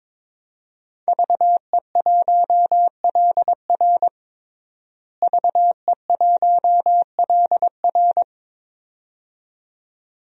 Morse code, heard as a keyed tone, about 22 wpm, 700 Hz. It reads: VE1LR VE1LR
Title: DXCC-100 Call Signs - 5 Characters - Repeated 2x - 22wpm